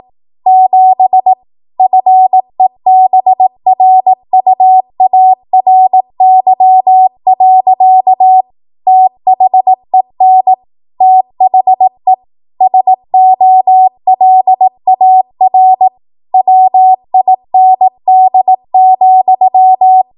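Morse code, heard as a keyed tone, about 18 wpm, 750 Hz. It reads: 7 FEBRUARY. THEN THE SOLAR WIND,
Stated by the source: Code practice at 18 WPM, a transition run bridging 15 to 20 WPM.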